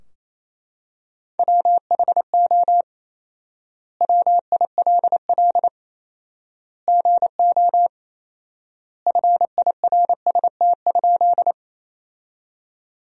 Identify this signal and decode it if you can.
Morse code, heard as a keyed tone, about 28 wpm, 700 Hz. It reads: WHO WILL GO FIRST?
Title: 4x Word-Spacing - Sentences from Top 100 Words - 28wpm